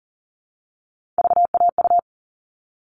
Morse code, heard as a keyed tone, about 40 wpm, 700 Hz. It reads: VAU
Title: All Cut Numbers - 3-Digit 40wpm